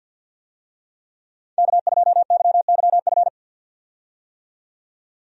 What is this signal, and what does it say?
Morse code, heard as a keyed tone, about 50 wpm, 700 Hz. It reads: K2YYF
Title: USA Call Signs - 5 Characters - 50wpm